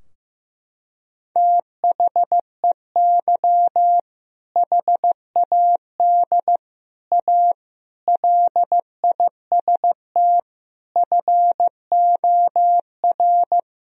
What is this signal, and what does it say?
Morse code, heard as a keyed tone, about 15 wpm, 700 Hz. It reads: THEY HAD A LIST FOR
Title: Sentences from Top 1000 Words - Review Part 2 - 15wpm